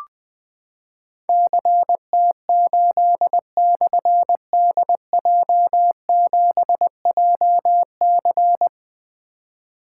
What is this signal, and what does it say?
Morse code, heard as a keyed tone, about 20 wpm, 700 Hz. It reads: CT8/DJ7JC
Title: Difficult Call Signs 20wpm